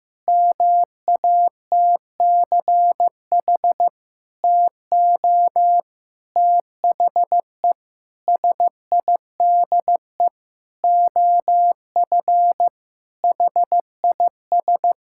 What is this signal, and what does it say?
Morse code, heard as a keyed tone, about 15 wpm, 700 Hz. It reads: MATCH TO THE SIDE OF HIS
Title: Sentences from Top 1600 Words 15wpm